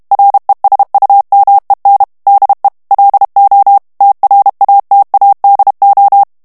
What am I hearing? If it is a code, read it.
Morse code, heard as a keyed tone, about 32 wpm, 800 Hz. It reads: RESUMEN DE LO TRATADO
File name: morse mp3 / ESPAGNOL TEXTE EN CLAIR 1 VITESSE 30 MOTS MINUTES